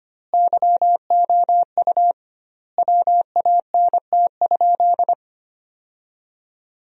Morse code, heard as a keyed tone, about 25 wpm, 700 Hz. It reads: YOU WANT?
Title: 2x Word-Spacing - Sentences from Top 200 Words - 25wpm